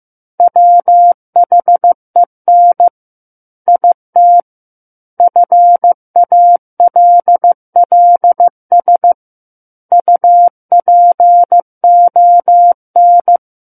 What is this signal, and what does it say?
Morse code, heard as a keyed tone, about 15 wpm, 700 Hz. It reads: WHEN IT FALLS UPON
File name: Chapter 6 - 15 WPM